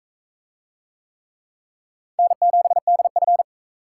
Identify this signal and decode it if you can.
Morse code, heard as a keyed tone, about 44 wpm, 700 Hz. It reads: N7DF